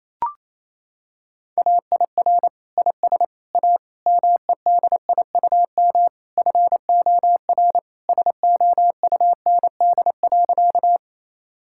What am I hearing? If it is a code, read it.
Morse code, heard as a keyed tone, about 28 wpm, 700 Hz. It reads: AIR IS A MEDIUM FOR SOUND.
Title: Sentences from Top 1000 Words - Review Part 6 - 28wpm